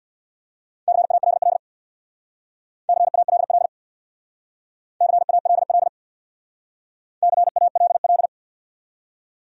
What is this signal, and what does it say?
Morse code, heard as a keyed tone, about 68 wpm, 700 Hz. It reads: CALL CALL CALL CALL